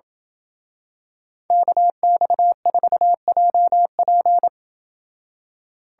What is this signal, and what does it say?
Morse code, heard as a keyed tone, about 27 wpm, 700 Hz. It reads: KX4JP